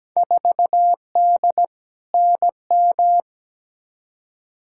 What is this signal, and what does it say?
Morse code, heard as a keyed tone, about 17 wpm, 700 Hz. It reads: 4D NM
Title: Field Day Exchanges Encore - 17wpm